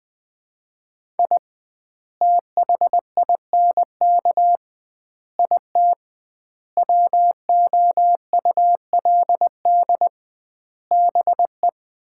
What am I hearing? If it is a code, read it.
Morse code, heard as a keyed tone, about 21 wpm, 700 Hz. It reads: I THINK IT WOULD BE
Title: 2x Word-Spacing - Sets of 5 Words - 20wpm